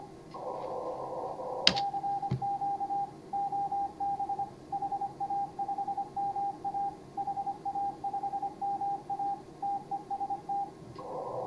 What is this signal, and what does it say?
Morse code, heard as a keyed tone, about 26 wpm, 810 Hz. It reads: CQ OB HA5MA HA5MA TEST